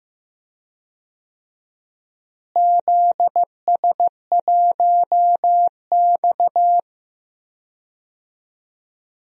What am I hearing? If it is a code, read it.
Morse code, heard as a keyed tone, about 15 wpm, 700 Hz. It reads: ZS1X
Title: DXCC-100 Call Signs - 4 Characters - Repeated 2x - 15wpm